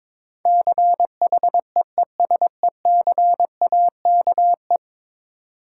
Morse code, heard as a keyed tone, about 22 wpm, 700 Hz. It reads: CHEESECAKE
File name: Difficult Words with Hints - 22wpm